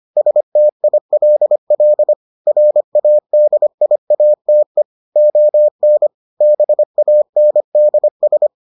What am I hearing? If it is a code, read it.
Morse code, heard as a keyed tone, about 25 wpm, 600 Hz. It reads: STILL RADIATE ON BANDS